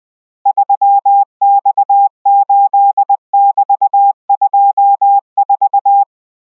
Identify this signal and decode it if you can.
Morse code, heard as a keyed tone, about 20 wpm, 800 Hz. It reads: 3X8<BT>24